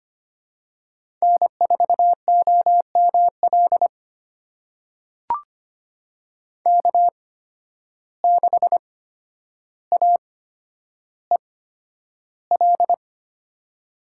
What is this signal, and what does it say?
Morse code, heard as a keyed tone, about 25 wpm, spaced as for 11 wpm, 700 Hz. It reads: N4OML K6AEL
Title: USA Call Signs - 5 Characters - 8x Character Spacing - 25wpm